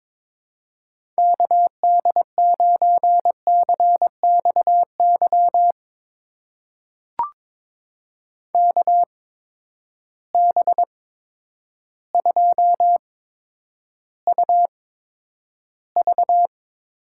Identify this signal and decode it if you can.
Morse code, heard as a keyed tone, about 22 wpm, spaced as for 10 wpm, 700 Hz. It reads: KD9CXY KB2UV
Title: USA Call Signs - 6 Characters - 8x Character Spacing - 22wpm